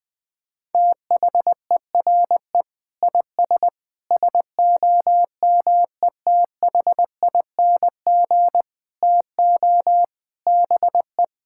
Morse code, heard as a keyed tone, about 20 wpm, 700 Hz. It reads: THERE IS SOMETHING TO BE
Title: Sentences with 7-Word Phrases - 20wpm